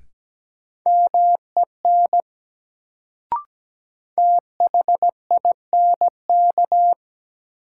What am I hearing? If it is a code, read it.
Morse code, heard as a keyed tone, about 17 wpm, 700 Hz. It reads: MEN THINK
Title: Sentences from Top 200 Words - New Word Drill - 17wpm